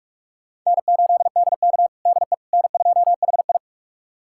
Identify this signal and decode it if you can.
Morse code, heard as a keyed tone, about 46 wpm, 700 Hz. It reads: N8DK DE N2HI
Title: USA Callsign DE Callsign - 4 Characters - 45wpm